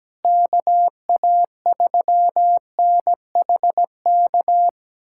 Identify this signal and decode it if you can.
Morse code, heard as a keyed tone, about 17 wpm, 700 Hz. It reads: KA3NHK
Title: USA Callsign DE Callsign 17wpm